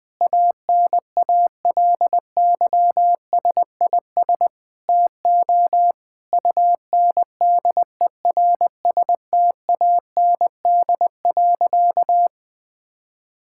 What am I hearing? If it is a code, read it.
Morse code, heard as a keyed tone, about 20 wpm, 700 Hz. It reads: ANALYSIS TO UNDERSTAND.
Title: Sentences from Top 1000 Words - Review Part 5 - 20wpm